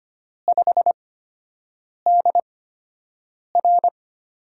Morse code, heard as a keyed tone, about 25 wpm, spaced as for 7 wpm, 700 Hz. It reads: 5DR
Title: USA Call Signs - 4 Characters - 8x Character Spacing - 25wpm